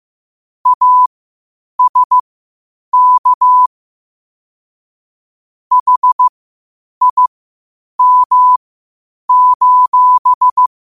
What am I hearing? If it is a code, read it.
Morse code, heard as a keyed tone, about 15 wpm, spaced as for 8 wpm, 1000 Hz. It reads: ASK HIM:
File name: qotdfiles / 8wpm